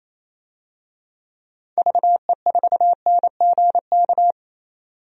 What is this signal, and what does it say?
Morse code, heard as a keyed tone, about 27 wpm, 700 Hz. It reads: VE4NGK
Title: DXCC-100 Call Signs - 6 Characters - 28wpm